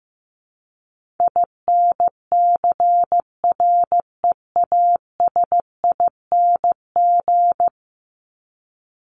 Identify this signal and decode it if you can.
Morse code, heard as a keyed tone, about 15 wpm, 700 Hz. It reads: INCREASING
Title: ING Suffix - Top 100 Words with Base Words 15wpm